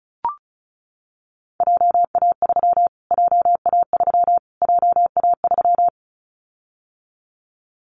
Morse code, heard as a keyed tone, about 34 wpm, 700 Hz. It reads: JA3 JA3 JA3
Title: Callsign Prefixes for Top 25 DXCC - Repeated 3x - 35wpm